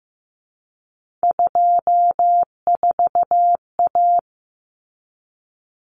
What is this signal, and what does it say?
Morse code, heard as a keyed tone, about 15 wpm, 700 Hz. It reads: 24A
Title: Common Cut Numbers - 3-Digit 15wpm